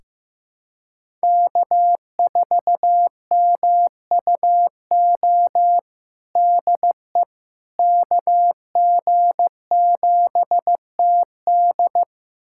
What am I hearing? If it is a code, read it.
Morse code, heard as a keyed tone, about 15 wpm, 700 Hz. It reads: K4MUO DE KG7TD
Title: USA Callsign DE Callsign - 5 Characters - 15wpm